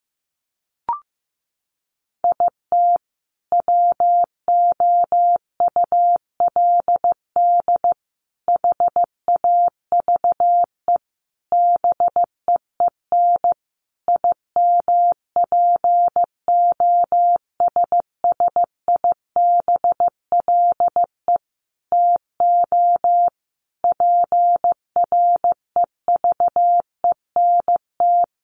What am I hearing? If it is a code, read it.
Morse code, heard as a keyed tone, about 15 wpm, 700 Hz. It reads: IT WOULD HAVE BEEN IMPOSSIBLE TO PREVENT
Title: Sentences from Top 1100 Words 15wpm